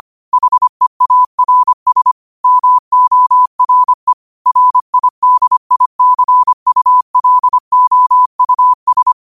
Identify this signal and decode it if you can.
Morse code, heard as a keyed tone, about 25 wpm, 1000 Hz. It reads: HEARS MORE RIDICULOUS
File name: qotdfiles / 25wpm